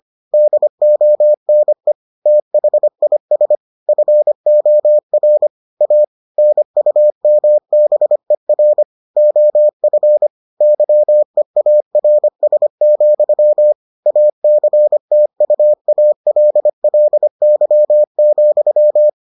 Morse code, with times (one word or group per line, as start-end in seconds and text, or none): 0.3-1.9 DONE
2.3-3.6 THIS
3.9-5.5 FOR
5.8-6.0 A
6.4-8.8 NUMBER
9.2-10.3 OF
10.6-13.7 YEARS,
14.1-19.1 ACTUALLY,